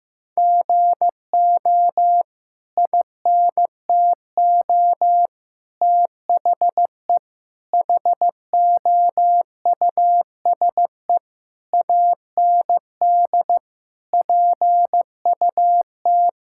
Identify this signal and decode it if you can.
Morse code, heard as a keyed tone, about 15 wpm, 700 Hz. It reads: GO INTO THE HOUSE AND PUT
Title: Sentences from Top 1000 Words - Review Part 5 - 15wpm